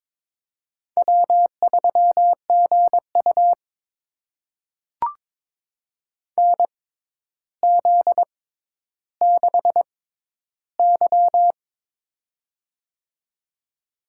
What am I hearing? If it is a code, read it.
Morse code, heard as a keyed tone, about 22 wpm, spaced as for 11 wpm, 700 Hz. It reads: W3GU NZ6Y
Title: USA Call Signs - 4 Characters - 6x Character Spacing - 22wpm